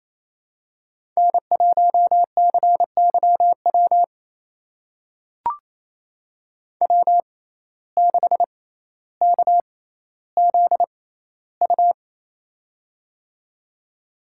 Morse code, heard as a keyed tone, about 28 wpm, spaced as for 14 wpm, 700 Hz. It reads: N1CYW W6KZU
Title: USA Call Signs - 5 Characters - 6x Character Spacing - 28wpm